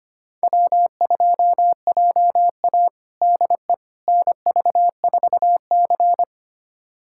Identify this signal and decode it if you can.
Morse code, heard as a keyed tone, about 25 wpm, 700 Hz. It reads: W2JA DE NV4C
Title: USA Callsign DE Callsign - 4 Characters - 25wpm